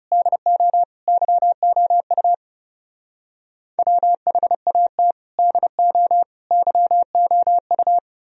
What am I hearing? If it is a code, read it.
Morse code, heard as a keyed tone, about 35 wpm, 700 Hz. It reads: DO YOU WHAT DO YOU
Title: Speed-Racing - Sets of 3 Words 30wpm